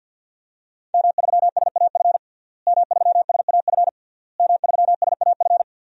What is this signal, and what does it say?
Morse code, heard as a keyed tone, about 49 wpm, 700 Hz. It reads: M3SAF M3SAF M3SAF
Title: DXCC-100 Call Signs - 5 Characters - Repeated 3x - 50wpm